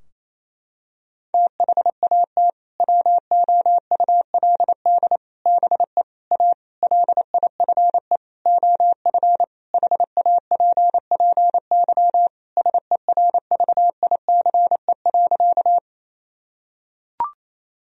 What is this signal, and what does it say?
Morse code, heard as a keyed tone, about 28 wpm, 700 Hz. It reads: THAT WOULD BE A LIFE OF HAPPY SERVICE.
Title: Sentences from Top 1000 Words - Review Part 4 - 28wpm